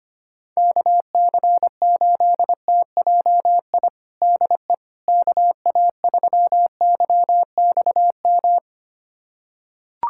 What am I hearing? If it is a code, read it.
Morse code, heard as a keyed tone, about 25 wpm, 700 Hz. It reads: KC8TJI DE KA3YXM
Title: USA Callsign DE Callsign - 6 Characters - 25wpm